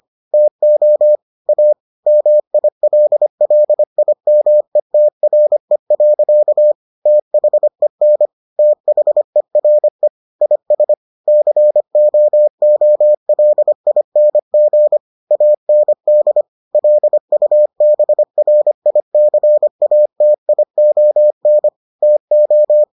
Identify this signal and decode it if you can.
Morse code, heard as a keyed tone, about 25 wpm, 600 Hz. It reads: TO A MILLIMETRE. THEN THERE IS COOLING AND LUBRICATION TO